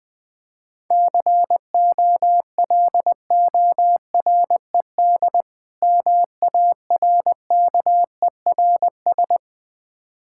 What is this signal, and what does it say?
Morse code, heard as a keyed tone, about 20 wpm, 700 Hz. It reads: COLORED MARKERS
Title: Difficult Words with Hints - 20wpm